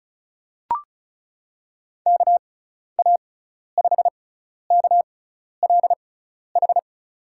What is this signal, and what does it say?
Morse code, heard as a keyed tone, about 34 wpm, spaced as for 12 wpm, 690 Hz. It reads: KA5KLH
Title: USA Call Signs - 6 Characters - 6x Character Spacing - 35wpm